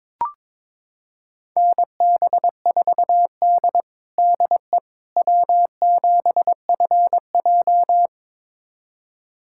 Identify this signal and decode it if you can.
Morse code, heard as a keyed tone, about 22 wpm, 700 Hz. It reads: NB4D DE W7FJ
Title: USA Callsign DE Callsign - 4 Characters - 22wpm